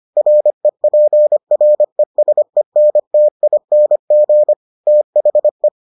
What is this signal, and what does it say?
Morse code, heard as a keyed tone, about 25 wpm, 600 Hz. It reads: REPRESENTING THE